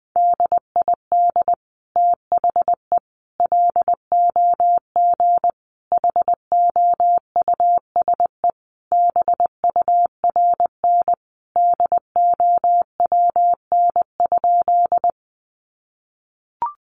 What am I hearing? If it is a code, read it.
Morse code, heard as a keyed tone, about 20 wpm, 700 Hz. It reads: DID THE LOG HOUSE BURN DOWN?
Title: Sentences from Top 1400 Words 20wpm